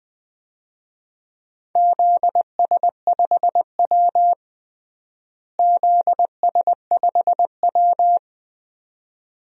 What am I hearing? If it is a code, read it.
Morse code, heard as a keyed tone, about 20 wpm, 700 Hz. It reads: ZS5W ZS5W
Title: DXCC-100 Call Signs - 4 Characters - Repeated 2x - 20wpm